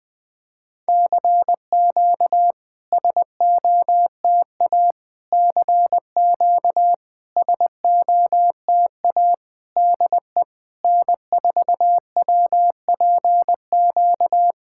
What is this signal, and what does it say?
Morse code, heard as a keyed tone, about 20 wpm, 700 Hz. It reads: CQ SOTA CQ SOTA DE N4WPQ
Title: SOTA QSO Practice 20wpm